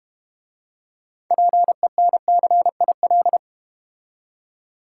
Morse code, heard as a keyed tone, about 32 wpm, 700 Hz. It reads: PENCIL